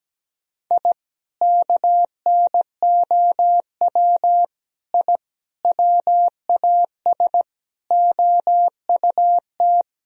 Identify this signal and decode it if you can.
Morse code, heard as a keyed tone, about 17 wpm, 700 Hz. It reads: I KNOW I WAS OUT
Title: Sentences from Top 1000 Words - Review Part 2 - 17wpm